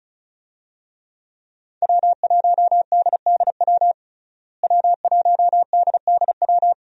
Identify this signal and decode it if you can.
Morse code, heard as a keyed tone, about 35 wpm, 700 Hz. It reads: W1DDW W1DDW
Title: USA Call Signs - 5 Characters - Repeated 2x - 35wpm